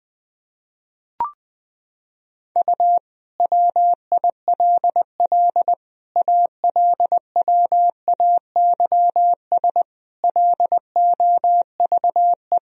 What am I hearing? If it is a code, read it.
Morse code, heard as a keyed tone, about 20 wpm, 700 Hz. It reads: U WILL ALWAYS LOVE
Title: Sentences from Top 400 Words and Abbreviations 20wpm